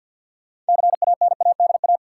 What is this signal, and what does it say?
Morse code, heard as a keyed tone, about 52 wpm, 700 Hz. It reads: CANADA
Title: Sets of 5 Words - New Word Drill - 50wpm